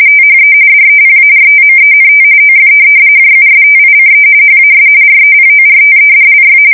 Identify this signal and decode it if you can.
RTTY, about 45.45 baud. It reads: THE DEPARTMENT OF JUSTICE HAS REPORTEDLY
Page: 170 Shift 2125/2295